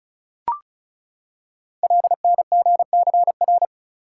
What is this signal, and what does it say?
Morse code, heard as a keyed tone, about 36 wpm, 700 Hz. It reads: LNGCR
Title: Code Groups - 5 Letters - 35wpm